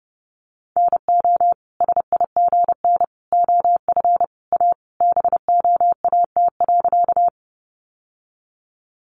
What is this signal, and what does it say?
Morse code, heard as a keyed tone, about 30 wpm, 700 Hz. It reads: NO SIGN OF A BOAT.
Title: Sentences from Top 1000 Words 30wpm